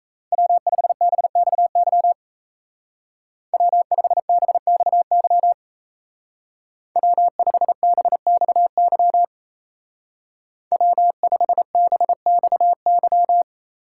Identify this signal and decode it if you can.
Morse code, heard as a keyed tone, about 34 wpm, 700 Hz. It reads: W5BXY W5BXY W5BXY W5BXY
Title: Speed Racing - USA 1x3 Call Signs 28wpm